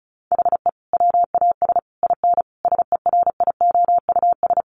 Morse code, heard as a keyed tone, about 34 wpm, 700 Hz. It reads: HE WAS IN SERIOUS
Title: Sentences from Top 1300 Words 35wpm